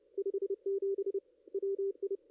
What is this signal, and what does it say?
Morse code, heard as a keyed tone, about 27 wpm, 440 Hz. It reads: 57 WI
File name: contests / ss-cw-k5tr-op-n5rz -2300